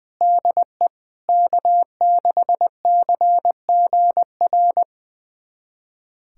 Morse code, heard as a keyed tone, about 21 wpm, 700 Hz. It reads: DE K6CGR
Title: USA Callsign DE Callsign - 5 Characters - 20wpm